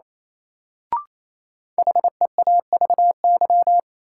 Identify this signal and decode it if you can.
Morse code, heard as a keyed tone, about 28 wpm, 700 Hz. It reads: HEAVY
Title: Sentences from Top 1100 Words - New Word Drill Alt - 28wpm